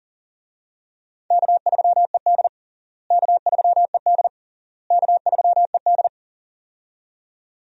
Morse code, heard as a keyed tone, about 40 wpm, 700 Hz. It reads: K3ED K3ED K3ED
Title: USA Call Signs - 4 Characters - Repeated 3x - 40wpm